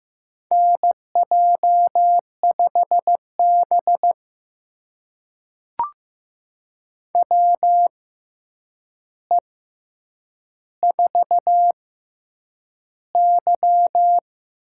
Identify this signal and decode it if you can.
Morse code, heard as a keyed tone, about 15 wpm, spaced as for 8 wpm, 700 Hz. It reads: NJ5B WE4Y
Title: USA Call Signs - 4 Characters - 6x Character Spacing - 15wpm